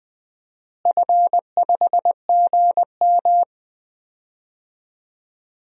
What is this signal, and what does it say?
Morse code, heard as a keyed tone, about 20 wpm, 700 Hz. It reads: F5GM
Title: DXCC-100 Call Signs - 4 Characters - Repeated 2x - 20wpm